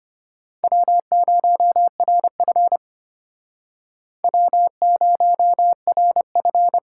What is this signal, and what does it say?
Morse code, heard as a keyed tone, about 33 wpm, 700 Hz. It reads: W0RF W0RF